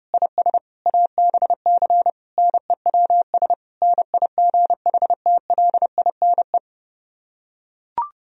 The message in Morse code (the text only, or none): IS ABC NEWS NIGHTLINE